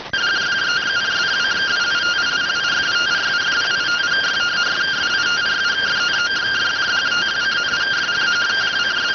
Two-tone teleprinter signal, about 170 Hz shift.